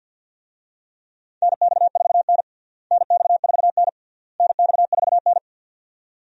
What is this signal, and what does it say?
Morse code, heard as a keyed tone, about 46 wpm, 700 Hz. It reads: NX4N NX4N NX4N
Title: USA Call Signs - 4 Characters - Repeated 3x - 50wpm